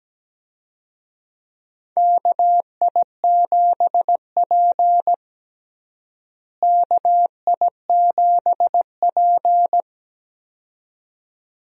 Morse code, heard as a keyed tone, about 17 wpm, 700 Hz. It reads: KI7P KI7P
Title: USA Call Signs - 4 Characters - Repeated 2x - 17wpm